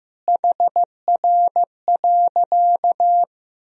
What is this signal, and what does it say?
Morse code, heard as a keyed tone, about 16 wpm, 700 Hz. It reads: HR.
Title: Sentences from Top 200 Words and Abbreviations 15wpm